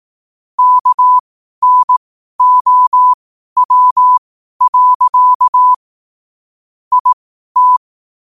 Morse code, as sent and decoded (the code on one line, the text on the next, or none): -.- -. --- .-- .-.-.- / .. -
KNOW. IT